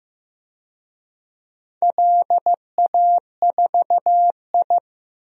Morse code, heard as a keyed tone, about 16 wpm, 700 Hz. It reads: LA4I
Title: DXCC-100 Call Signs - 4 Characters - Repeated 3x - 15wpm